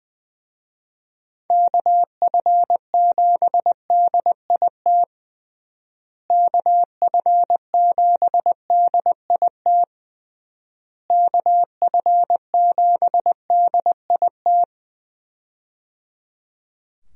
Morse code, heard as a keyed tone, about 21 wpm, 700 Hz. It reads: KF7DIT KF7DIT KF7DIT
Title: USA Call Signs - 6 Characters - Repeated 3x - 20wpm